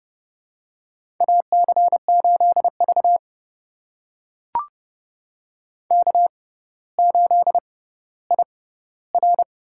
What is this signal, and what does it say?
Morse code, heard as a keyed tone, about 30 wpm, spaced as for 15 wpm, 700 Hz. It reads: AC8V K8IR
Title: USA Call Signs - 4 Characters - 6x Character Spacing - 30wpm